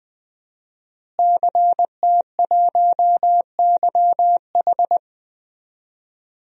Morse code, heard as a keyed tone, about 21 wpm, 700 Hz. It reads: CT1YH
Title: DXCC-100 Call Signs - 5 Characters - Repeated 2x - 20wpm